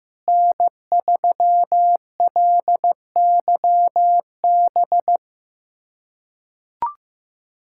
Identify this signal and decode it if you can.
Morse code, heard as a keyed tone, about 15 wpm, 700 Hz. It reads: N3LYB
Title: USA Callsign DE Callsign 15wpm